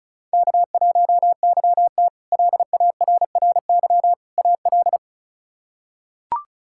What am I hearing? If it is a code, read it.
Morse code, heard as a keyed tone, about 36 wpm, 700 Hz. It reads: K1YT LARRY AL
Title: K1USN SST QSO Practice - 35wpm